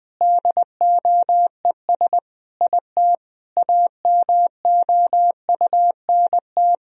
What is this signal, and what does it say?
Morse code, heard as a keyed tone, about 20 wpm, 700 Hz. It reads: DOES IT AMOUNT
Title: Sentences from Top 1000 Words - Review Part 4 - 20wpm